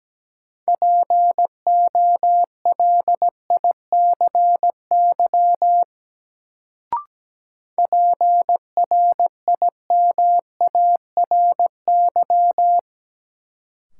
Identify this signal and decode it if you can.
Morse code, heard as a keyed tone, about 17 wpm, 700 Hz. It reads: POLICY PRIMARY
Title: Sentences from Top 1000 Words - Review Part 5 - New Word Drill - 17wpm